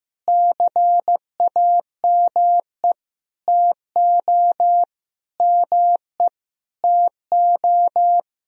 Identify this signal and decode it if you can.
Morse code, heard as a keyed tone, about 16 wpm, 700 Hz. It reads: CAME TO ME TO
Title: Sentences from Top 200 Words and Abbreviations 15wpm